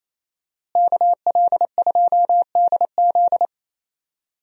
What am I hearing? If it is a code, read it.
Morse code, heard as a keyed tone, about 28 wpm, 700 Hz. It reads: KL2DZ
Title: USA Call Signs - 5 Characters - Repeated 2x - 28wpm